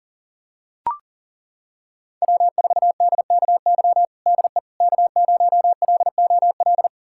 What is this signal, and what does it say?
Morse code, heard as a keyed tone, about 40 wpm, 700 Hz. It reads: W4DKY DE K0LOL
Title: USA Callsign DE Callsign - 5 Characters - 40wpm